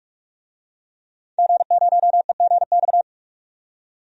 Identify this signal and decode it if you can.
Morse code, heard as a keyed tone, about 44 wpm, 700 Hz. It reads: G0EGX
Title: DXCC-100 Call Signs - 5 Characters - 45wpm